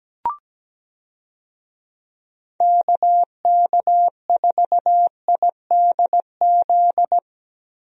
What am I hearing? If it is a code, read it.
Morse code, heard as a keyed tone, about 17 wpm, 700 Hz. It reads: KK4IDZ
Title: USA Call Signs - 6 Characters - Repeated 2x - 17wpm